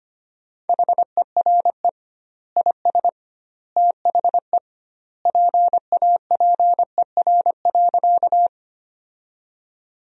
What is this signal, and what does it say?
Morse code, heard as a keyed tone, about 25 wpm, 700 Hz. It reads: HERE IS THE PAPER.
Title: 2x Word-Spacing - Sentences from Top 300 Words - 25wpm